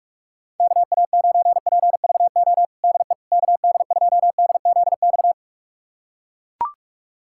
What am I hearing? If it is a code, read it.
Morse code, heard as a keyed tone, about 41 wpm, 700 Hz. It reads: KA9PVO DE KD2DZX
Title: USA Callsign DE Callsign - 6 Characters - 45wpm